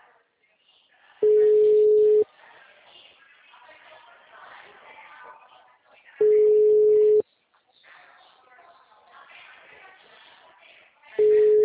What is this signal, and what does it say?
Ringback tone (Europe, 425 Hz), with speech.